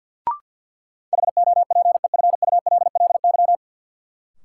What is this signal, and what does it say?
Morse code, heard as a keyed tone, about 50 wpm, 700 Hz. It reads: HOPEFULLY
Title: Sentences from Top 1500 Words - New Word Drill - 50wpm